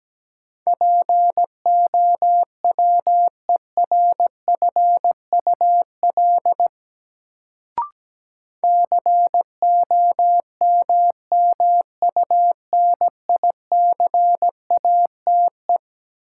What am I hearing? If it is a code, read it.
Morse code, heard as a keyed tone, about 17 wpm, 700 Hz. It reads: POWERFUL COMMUNICATE
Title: Sentences from Top 1300 Words - New Word Drill Alt - 17wpm